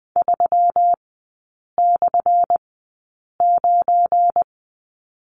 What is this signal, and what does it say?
Morse code, heard as a keyed tone, about 20 wpm, spaced as for 8 wpm, 690 Hz. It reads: 3/9
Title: 212 Lesson 34e - Number 8 - Three Characters 20wpm